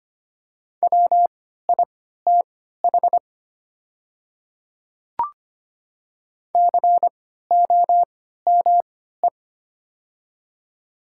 Morse code, heard as a keyed tone, about 25 wpm, spaced as for 14 wpm, 700 Hz. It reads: WITH COME
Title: 3x Character Spacing - Top 100 Words - 25wpm